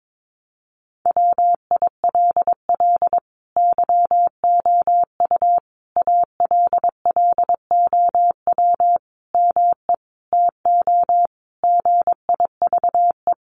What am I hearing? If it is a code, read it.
Morse code, heard as a keyed tone, about 22 wpm, 700 Hz. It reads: WILL YOU ALLOW ME TO GIVE
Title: Sentences from Top 1200 Words 22wpm